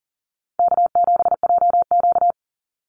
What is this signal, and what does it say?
Morse code, heard as a keyed tone, about 40 wpm, 700 Hz. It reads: K7JQ
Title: Easy Call Signs Repeated 3x - 40wpm